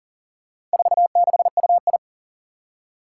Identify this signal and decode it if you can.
Morse code, heard as a keyed tone, about 37 wpm, 700 Hz. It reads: 46UI
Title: Four Challenging Letter-Number Combinations 40wpm